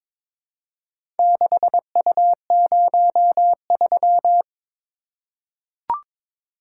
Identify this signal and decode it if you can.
Morse code, heard as a keyed tone, about 22 wpm, 700 Hz. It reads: 6U03
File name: practice / 22wpm / Four Letter-Number Combinations - 22wpm